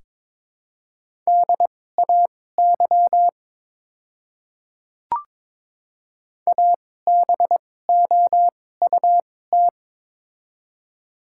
Morse code, heard as a keyed tone, about 22 wpm, spaced as for 16 wpm, 700 Hz. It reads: DAY ABOUT
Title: 2x Character Spacing - Top 500 Words - 22wpm